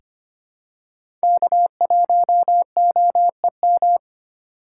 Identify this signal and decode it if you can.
Morse code, heard as a keyed tone, about 25 wpm, 690 Hz. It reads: K1OEM
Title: USA Call Signs - 5 Characters - 6x Character Spacing - 25wpm